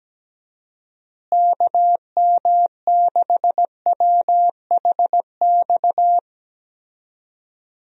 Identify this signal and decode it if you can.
Morse code, heard as a keyed tone, about 17 wpm, 700 Hz. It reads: KM6WHX
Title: USA Call Signs - 6 Characters - Repeated 3x - 17wpm